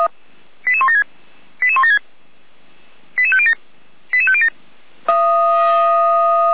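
Five-tone selective call (ZVEI), heard as quick runs of five tones, tones of about 80 ms. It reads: 88176 88176 88507 88507
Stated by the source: ZVEI five-tone selective call received on SDR.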